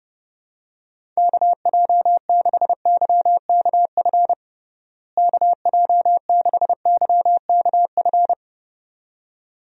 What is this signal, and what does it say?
Morse code, heard as a keyed tone, about 32 wpm, 700 Hz. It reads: KJ6YKF KJ6YKF
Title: USA Call Signs - 6 Characters - Repeated 2x - 30wpm